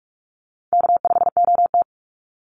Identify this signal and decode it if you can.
Morse code, heard as a keyed tone, about 44 wpm, 690 Hz. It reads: K5OT
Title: NA SOTA Chaser Callsigns 45wpm